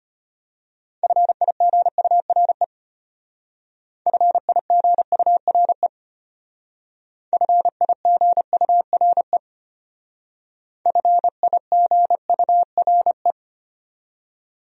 Morse code, heard as a keyed tone, about 32 wpm, 700 Hz. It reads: FIGURE FIGURE FIGURE FIGURE